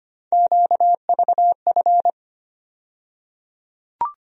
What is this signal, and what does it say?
Morse code, heard as a keyed tone, about 25 wpm, 700 Hz. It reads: QVF
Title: USA Call Sign Postfixes - 3 Letters 25wpm